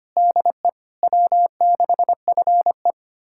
Morse code, heard as a keyed tone, about 25 wpm, 700 Hz. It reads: DE W6FE
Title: USA Callsign DE Callsign - 4 Characters - 25wpm